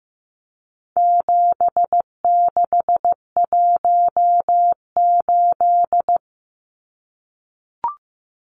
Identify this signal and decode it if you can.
Morse code, heard as a keyed tone, about 15 wpm, 700 Hz. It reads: 7618
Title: Numbers - 4-Digit 15wpm